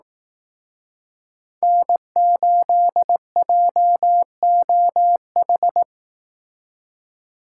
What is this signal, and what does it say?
Morse code, heard as a keyed tone, about 18 wpm, 700 Hz. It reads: N8JOH
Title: Speed Racing - USA 1x3 Call Signs 15wpm